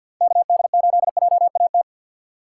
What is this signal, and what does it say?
Morse code, heard as a keyed tone, about 53 wpm, 700 Hz. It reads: KD8JAT
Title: USA Call Signs - 6 Characters - 3x Character Spacing - 50wpm